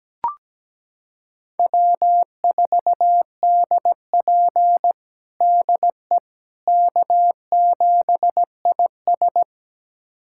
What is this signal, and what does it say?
Morse code, heard as a keyed tone, about 17 wpm, 700 Hz. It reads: W4DP DE K7IS
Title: USA Callsign DE Callsign - 4 Characters - 17wpm